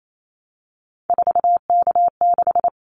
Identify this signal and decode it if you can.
Morse code, heard as a keyed tone, about 28 wpm, 700 Hz. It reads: 4K6
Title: Callsign Prefixes for Top 100 DXCC - 28wpm